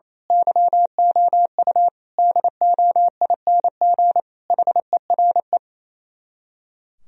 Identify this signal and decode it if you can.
Morse code, heard as a keyed tone, about 26 wpm, 700 Hz. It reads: YOU DOING HERE
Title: Speed-Racing - Sets of 5 Words 28wpm